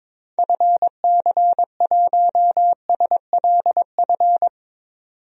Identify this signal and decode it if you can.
Morse code, heard as a keyed tone, about 21 wpm, 700 Hz. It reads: FC1SLF
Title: DXCC-100 Call Signs - 6 Characters - 22wpm